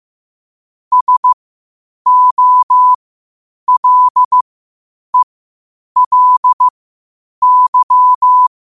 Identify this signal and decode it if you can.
Morse code, heard as a keyed tone, about 15 wpm, spaced as for 8 wpm, 1000 Hz. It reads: SOLELY
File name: qotdfiles / 8wpm